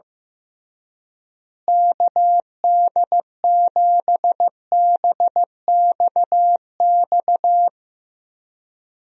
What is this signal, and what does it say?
Morse code, heard as a keyed tone, about 15 wpm, 700 Hz. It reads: KD7BXX
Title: USA Call Signs - 6 Characters - Repeated 3x - 15wpm